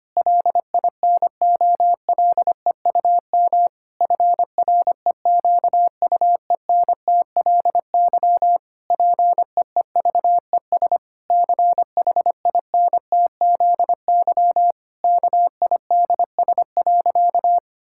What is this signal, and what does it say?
Morse code, heard as a keyed tone, about 25 wpm, 700 Hz. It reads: LINOLEUM FREQUENTLY PEEVES CHINTZY KIDS.
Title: Pangram Challenge v2 - 25wpm